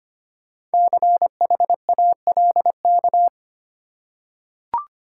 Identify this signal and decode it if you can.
Morse code, heard as a keyed tone, about 25 wpm, 700 Hz. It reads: CHALK
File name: practice / 25wpm / Easy Words with Hints - 25wpm